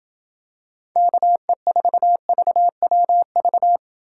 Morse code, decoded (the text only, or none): KE4VWV